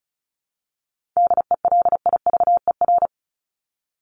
Morse code, heard as a keyed tone, about 34 wpm, 700 Hz. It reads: DELIVER
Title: Top 1751-2000 Words 35wpm